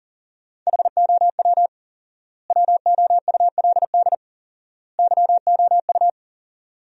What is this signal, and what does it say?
Morse code, heard as a keyed tone, about 40 wpm, 700 Hz. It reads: HOW WOULD YOU
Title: 4x Word-Spacing - Sentences from Top 100 Words - 40wpm